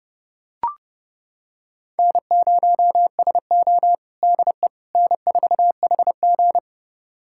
Morse code, heard as a keyed tone, about 30 wpm, 700 Hz. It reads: N0SO DE N4HG
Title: USA Callsign DE Callsign - 4 Characters - 30wpm